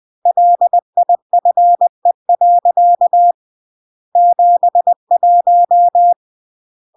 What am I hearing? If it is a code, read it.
Morse code, heard as a keyed tone, about 20 wpm, 700 Hz. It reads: LIFE. 71
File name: CW / jodorowsky / 82 commandments 20-20